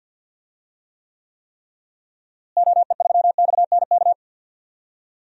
Morse code, heard as a keyed tone, about 52 wpm, 700 Hz. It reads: OE3XNK
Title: DXCC-100 Call Signs - 6 Characters - Repeated 3x - 50wpm